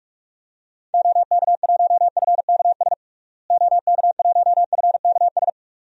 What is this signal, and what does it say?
Morse code, heard as a keyed tone, about 43 wpm, 700 Hz. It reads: OK1FKS OK1FKS
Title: DXCC-100 Call Signs - 6 Characters - Repeated 2x - 45wpm